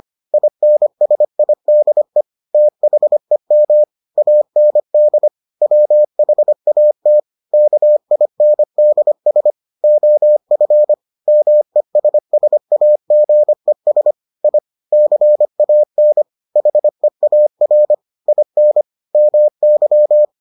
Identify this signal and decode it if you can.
Morse code, heard as a keyed tone, about 25 wpm, 600 Hz. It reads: INSIDE THEM AND WHAT KINDS OF MESSAGES I CAN HEAR IN MY